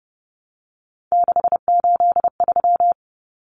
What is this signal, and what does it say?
Morse code, heard as a keyed tone, about 30 wpm, 700 Hz. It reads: 683
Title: Numbers - 3-Digit 30wpm